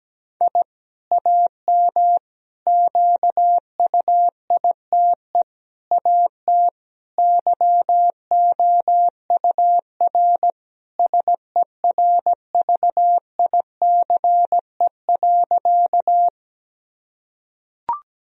Morse code, heard as a keyed tone, about 17 wpm, 700 Hz. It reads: I AM QUITE AT YOUR SERVICE.
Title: Sentences from Top 1000 Words - Review Part 4 - 17wpm